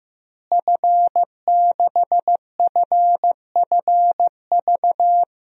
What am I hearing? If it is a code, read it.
Morse code, heard as a keyed tone, about 15 wpm, 700 Hz. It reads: F6FFV
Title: DXCC-100 Call Signs - 5 Characters - Repeated 2x - 15wpm